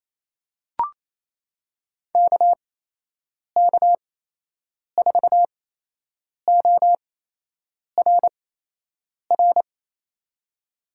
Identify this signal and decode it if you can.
Morse code, heard as a keyed tone, about 28 wpm, spaced as for 8 wpm, 700 Hz. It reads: KK4ORR
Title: USA Call Signs - 6 Characters - 8x Character Spacing - 28wpm